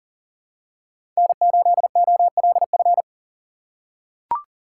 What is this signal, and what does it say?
Morse code, heard as a keyed tone, about 40 wpm, 700 Hz. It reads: N8OLF
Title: USA Call Signs - 5 Characters - 3x Character Spacing - 40wpm